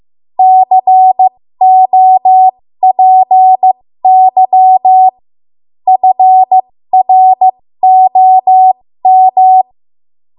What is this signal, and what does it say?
Morse code, heard as a keyed tone, about 15 wpm, spaced as for 13 wpm, 750 Hz. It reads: COPY FROM
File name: Morse / 13WPM